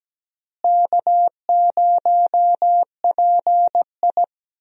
Morse code, heard as a keyed tone, about 17 wpm, 700 Hz. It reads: K0PI
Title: USA Call Signs - 4 Characters - Repeated 3x - 17wpm